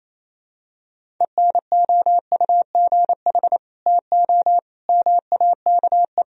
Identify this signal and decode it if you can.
Morse code, heard as a keyed tone, about 28 wpm, 700 Hz. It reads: ENOUGH TO MAKE
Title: Sentences from Top 1000 Words - Review Part 1 - 28wpm